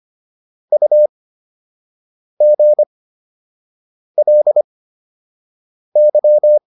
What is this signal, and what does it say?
Morse code, heard as a keyed tone, about 25 wpm, spaced as for 6 wpm, 600 Hz. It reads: UGLY